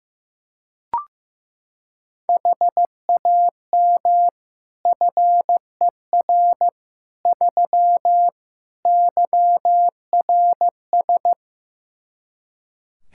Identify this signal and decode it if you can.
Morse code, heard as a keyed tone, about 15 wpm, 700 Hz. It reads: HAM FER 3 YRS
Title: Practice QSO - Exchange 3 - More Info - 15wpm